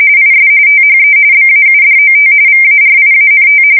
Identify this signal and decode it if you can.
RTTY, about 45.45 baud. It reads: IS THE BAUDOT VARIANT
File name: fsk170 / teletype-corp